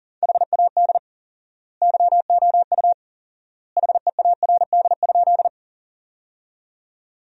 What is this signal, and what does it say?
Morse code, heard as a keyed tone, about 40 wpm, 700 Hz. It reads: HAD YOU HEARD?
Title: 4x Word-Spacing - Sentences from Top 400 Words - 40wpm